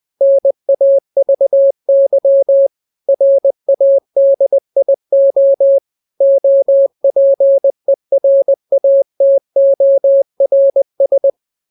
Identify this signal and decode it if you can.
Morse code, heard as a keyed tone, about 20 wpm, 550 Hz. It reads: NAVY RADIO OPERATORS
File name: jokes / joke 2 20 0000